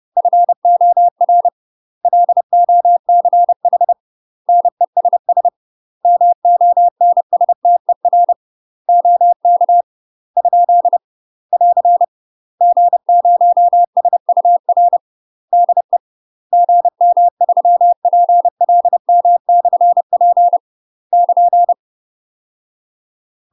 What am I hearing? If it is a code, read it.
Morse code, heard as a keyed tone, about 30 wpm, 700 Hz. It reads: FOR LOCH NESS MONSTER OK ? <AR> G0SUR DE GM3PLM/P <KN>
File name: UK MASTER QSOs TRAINING / 30WPM / UK MASTER QSOs 0034